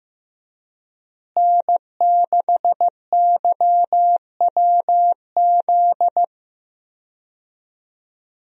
Morse code, heard as a keyed tone, about 15 wpm, 700 Hz. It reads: N6YWZ
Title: USA Call Signs - 5 Characters - Repeated 2x - 15wpm